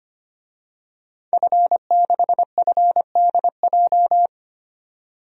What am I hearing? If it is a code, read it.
Morse code, heard as a keyed tone, about 25 wpm, 700 Hz. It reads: F6FDJ
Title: DXCC-100 Call Signs - 5 Characters - 25wpm